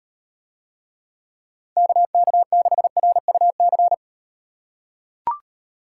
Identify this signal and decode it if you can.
Morse code, heard as a keyed tone, about 38 wpm, 700 Hz. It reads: KK6RUC